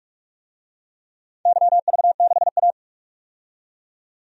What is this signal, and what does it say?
Morse code, heard as a keyed tone, about 44 wpm, 700 Hz. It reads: YV6A
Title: DXCC-100 Call Signs - 4 Characters - 45wpm